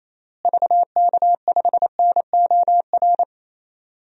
Morse code, heard as a keyed tone, about 28 wpm, 700 Hz. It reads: VK5NOR
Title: DXCC-100 Call Signs - 6 Characters - Repeated 3x - 28wpm